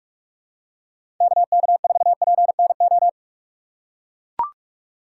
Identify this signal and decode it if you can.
Morse code, heard as a keyed tone, about 44 wpm, 700 Hz. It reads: KK4PNO
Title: USA Call Signs - 6 Characters - 6x Character Spacing - 45wpm